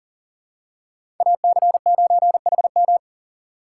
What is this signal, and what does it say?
Morse code, heard as a keyed tone, about 40 wpm, 700 Hz. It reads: AC9HM
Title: USA Call Signs - 5 Characters - 3x Character Spacing - 40wpm